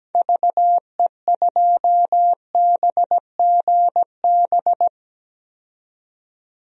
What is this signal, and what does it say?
Morse code, heard as a keyed tone, about 17 wpm, 700 Hz. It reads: VE2BGB
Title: DXCC-100 Call Signs - 6 Characters - 17wpm